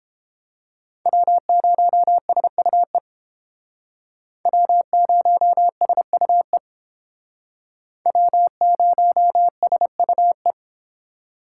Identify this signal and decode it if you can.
Morse code, heard as a keyed tone, about 30 wpm, 700 Hz. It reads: W0SUE W0SUE W0SUE